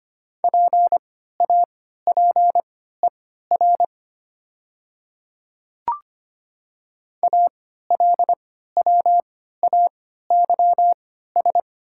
Morse code, heard as a keyed tone, about 25 wpm, spaced as for 14 wpm, 700 Hz. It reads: PAPER ALWAYS
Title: 3x Character Spacing - Top 300 Words - 25wpm